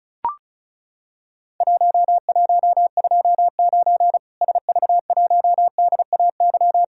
Morse code, heard as a keyed tone, about 36 wpm, 700 Hz. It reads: 1129 SV1DAY
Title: Worked All Europe DX Contest - QTC Exchange - 35wpm